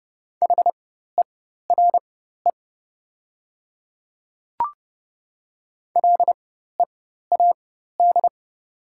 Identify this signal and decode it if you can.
Morse code, heard as a keyed tone, about 30 wpm, spaced as for 14 wpm, 700 Hz. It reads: HERE LEAD
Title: 4x Character Spacing - Top 500 Words - 30wpm